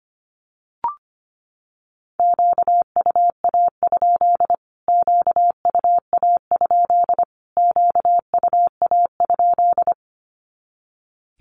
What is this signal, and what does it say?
Morse code, heard as a keyed tone, about 25 wpm, 700 Hz. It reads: QUA? QUA? QUA?
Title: Q-Codes Repeated 3x 25wpm